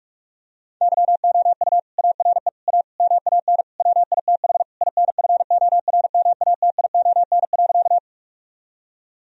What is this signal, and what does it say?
Morse code, heard as a keyed tone, about 46 wpm, 700 Hz. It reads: YOU ARE A MAN WITH INFORMATION.
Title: Sentences from Top 1000 Words - Review Part 3 - 45wpm